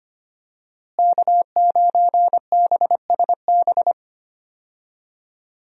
Morse code, heard as a keyed tone, about 25 wpm, 700 Hz. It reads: K9BSB
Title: USA Call Signs - 5 Characters - Repeated 3x - 25wpm